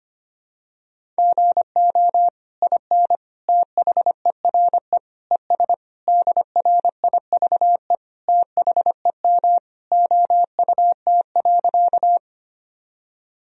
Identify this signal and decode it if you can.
Morse code, heard as a keyed tone, about 25 wpm, 700 Hz. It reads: GO IN THERE ES DRIVE THEM OUT.
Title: Sentences from Top 500 Words and Abbreviations 25wpm